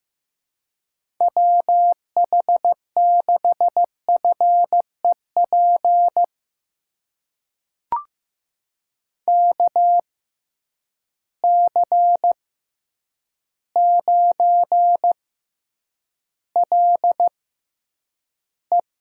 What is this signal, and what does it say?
Morse code, heard as a keyed tone, about 15 wpm, spaced as for 8 wpm, 700 Hz. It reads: WH6FEP KC9LE
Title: USA Call Signs - 6x Character Spacing - 15wpm